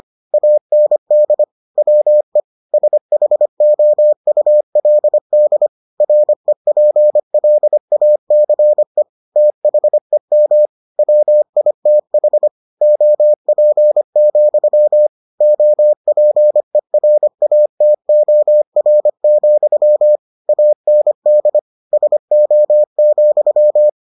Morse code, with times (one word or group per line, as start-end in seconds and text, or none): 0.3-1.4 AND
1.8-2.4 WE
2.7-5.7 SHOULD
6.0-9.0 REPLACE
9.4-10.7 THEM
11.0-12.5 WITH
12.8-15.1 OP,
15.4-20.2 OPERATOR,
20.5-21.6 AND
21.9-23.9 SO,